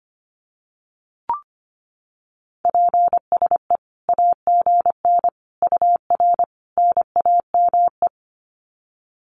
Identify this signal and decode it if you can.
Morse code, heard as a keyed tone, about 25 wpm, 700 Hz. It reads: PSE AGN UR NAME
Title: Common QSO Phrases 25wpm